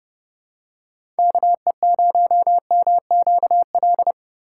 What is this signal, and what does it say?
Morse code, heard as a keyed tone, about 32 wpm, 700 Hz. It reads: KE0MQL
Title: USA Call Signs - 6 Characters - 4x Character Spacing - 30wpm